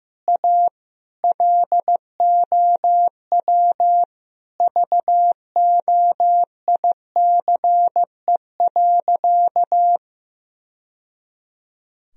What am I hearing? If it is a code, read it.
Morse code, heard as a keyed tone, about 15 wpm, 700 Hz. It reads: A LOW VOICE.
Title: Sentences from Top 1000 Words - Review Part 1 - 15wpm